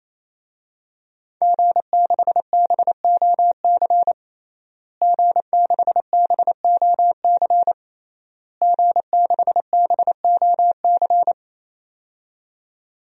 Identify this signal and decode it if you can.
Morse code, heard as a keyed tone, about 28 wpm, 700 Hz. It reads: G6BOC G6BOC G6BOC
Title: DXCC-100 Call Signs - 5 Characters - Repeated 3x - 28wpm